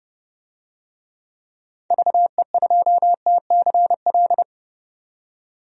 Morse code, heard as a keyed tone, about 30 wpm, 700 Hz. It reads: VE2TCL
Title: DXCC-100 Call Signs - 6 Characters - 30wpm